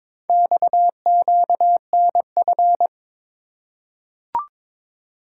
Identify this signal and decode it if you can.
Morse code, heard as a keyed tone, about 22 wpm, 700 Hz. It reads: XQNF
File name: practice / 22wpm / Four Letter-Number Combinations - 22wpm